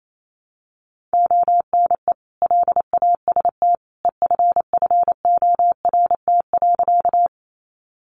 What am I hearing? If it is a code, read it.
Morse code, heard as a keyed tone, about 28 wpm, 700 Hz. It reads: ONE LAST EFFORT.
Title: Sentences from Top 1200 Words 28wpm